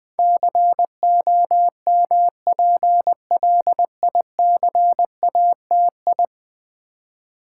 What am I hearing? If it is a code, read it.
Morse code, heard as a keyed tone, about 20 wpm, 700 Hz. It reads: COMPLICATI
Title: Build A Word - 5 Letters At A Time - 12-Letter Words - 20wpm